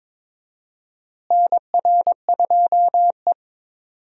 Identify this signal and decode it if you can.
Morse code, heard as a keyed tone, about 22 wpm, 700 Hz. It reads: NR2E
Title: USA Call Signs - 4 Characters - Repeated 3x - 22wpm